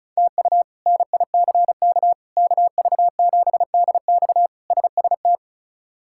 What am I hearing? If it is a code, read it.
Morse code, heard as a keyed tone, about 36 wpm, 700 Hz. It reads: TU NICK KV7DX SST
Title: K1USN SST QSO Practice - 35wpm